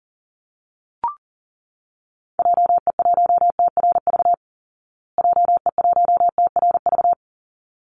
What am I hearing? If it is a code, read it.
Morse code, heard as a keyed tone, about 40 wpm, 700 Hz. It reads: JE1TRV JE1TRV
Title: Call Signs Repeated 3x - 40wpm